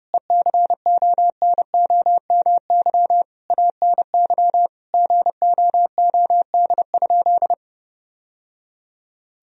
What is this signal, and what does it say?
Morse code, heard as a keyed tone, about 30 wpm, 700 Hz. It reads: ECONOMY ANY GOOD?
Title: Sentences from Top 1000 Words - Review Part 4 - 30wpm